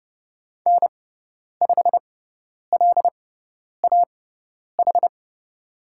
Morse code, heard as a keyed tone, about 30 wpm, spaced as for 10 wpm, 700 Hz. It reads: N5LAH
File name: Lesson 17g - Letter u - Call Signs 30wpm